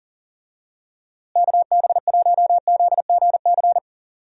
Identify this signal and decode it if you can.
Morse code, heard as a keyed tone, about 38 wpm, 700 Hz. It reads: KB1ZGC
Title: USA Call Signs - 6 Characters - Repeated 2x - 40wpm